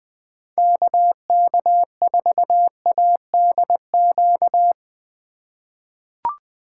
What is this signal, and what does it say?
Morse code, heard as a keyed tone, about 20 wpm, 700 Hz. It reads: KK4ADQ
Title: 2023 POTA Hunter Call Signs - 20wpm